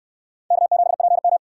Morse code, heard as a keyed tone, about 62 wpm, 700 Hz. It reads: K7PM